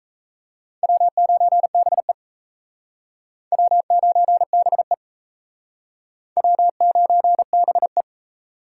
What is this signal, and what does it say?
Morse code, heard as a keyed tone, about 37 wpm, 700 Hz. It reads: W9BE W9BE W9BE